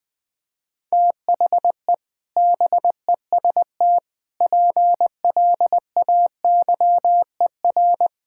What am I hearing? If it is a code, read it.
Morse code, heard as a keyed tone, about 20 wpm, 700 Hz. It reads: THE BEST PLAYER
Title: Sets of 6 Words - 20wpm